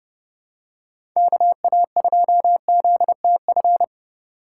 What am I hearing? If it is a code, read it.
Morse code, heard as a keyed tone, about 30 wpm, 700 Hz. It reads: KA2ZTF
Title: USA Call Signs - 6x Character Spacing - 30wpm